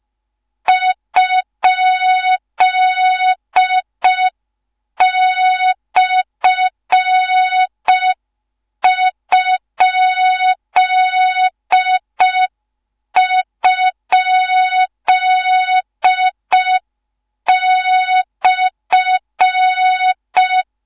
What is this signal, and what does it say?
Morse code, heard as a keyed tone, about 5 wpm, 3700 Hz. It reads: ?/??/
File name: Morze-HA2MN / hang / 13IRASJELEK1 HA2MN